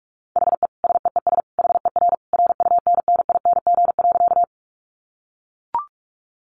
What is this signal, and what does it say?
Morse code, heard as a keyed tone, about 44 wpm, 700 Hz. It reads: HE SEES HER RUNNING.
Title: Additional Easy Phrases 45wpm